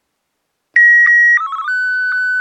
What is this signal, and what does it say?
SSTV (slow-scan television), Martin 1.